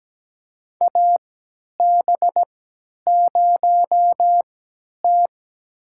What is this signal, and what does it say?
Morse code, heard as a keyed tone, about 17 wpm, spaced as for 10 wpm, 700 Hz. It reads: AB0T
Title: USA Call Signs - 4 Characters - 3x Character Spacing - 17wpm